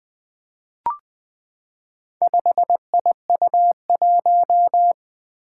Morse code, heard as a keyed tone, about 20 wpm, 700 Hz. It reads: 5IU1
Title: Four Challenging Letter-Number Combinations 20wpm